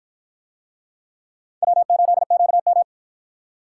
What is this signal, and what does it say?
Morse code, heard as a keyed tone, about 51 wpm, 700 Hz. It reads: W8QM